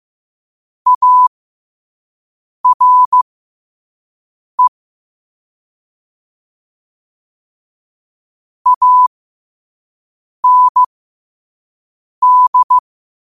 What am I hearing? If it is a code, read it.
Morse code, heard as a keyed tone, about 15 wpm, spaced as for 5 wpm, 1000 Hz. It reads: ARE AND
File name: 2024-10-07-5wpm